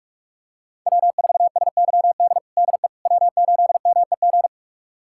Morse code, heard as a keyed tone, about 46 wpm, 700 Hz. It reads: W4SYD DE W8MEG
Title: USA Callsign DE Callsign - 5 Characters - 45wpm